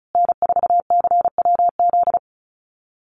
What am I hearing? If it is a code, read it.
Morse code, heard as a keyed tone, about 34 wpm, 700 Hz. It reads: N4CWZ
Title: USA Call Signs Repeated 3x 35wpm